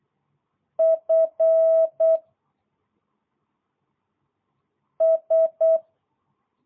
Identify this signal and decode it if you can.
Morse code, heard as a keyed tone, about 8 wpm, 650 Hz. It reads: F S